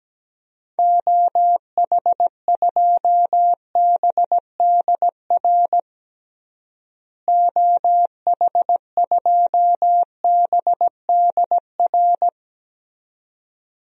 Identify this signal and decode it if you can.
Morse code, heard as a keyed tone, about 17 wpm, 700 Hz. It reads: OH2BDR OH2BDR
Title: DXCC-100 Call Signs - 6 Characters - Repeated 2x - 17wpm